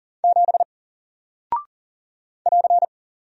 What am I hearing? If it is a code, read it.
Morse code, heard as a keyed tone, about 41 wpm, 700 Hz. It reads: 7 <AR>
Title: Single Character (Letter:Number:Punctuation:Prosign) 40wpm